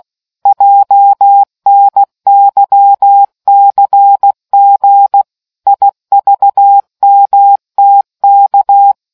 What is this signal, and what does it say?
Morse code, heard as a keyed tone, about 16 wpm, 800 Hz. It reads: JNYCG IVMTK